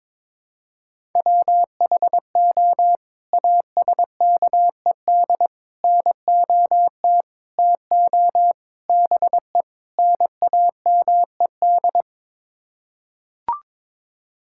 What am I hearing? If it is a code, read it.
Morse code, heard as a keyed tone, about 22 wpm, 700 Hz. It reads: WHO ASKED NOT TO BE NAMED
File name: practice / 22wpm / Sets of 6 Words - 22wpm